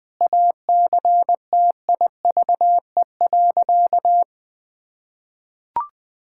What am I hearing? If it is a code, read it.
Morse code, heard as a keyed tone, about 20 wpm, 700 Hz. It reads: ACTIVE.
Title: Sentences from Top 1000 Words - Review Part 6 - 20wpm